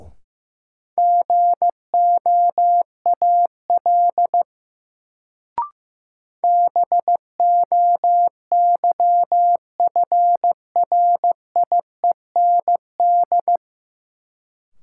Morse code, heard as a keyed tone, about 15 wpm, 700 Hz. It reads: GOAL BOYFRIEND
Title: Sentences from Top 1000 Words - Review Part 5 - New Word Drill - 15wpm